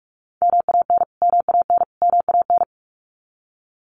Morse code, heard as a keyed tone, about 44 wpm, 700 Hz. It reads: MAN MAN MAN
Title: Top 500 Words Repeated 45wpm